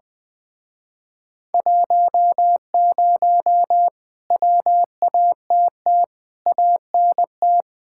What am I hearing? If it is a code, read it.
Morse code, heard as a keyed tone, about 20 wpm, 700 Hz. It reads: 10 WATT ANT
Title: Speed-Racing Common QSO Phrases Encore - 20wpm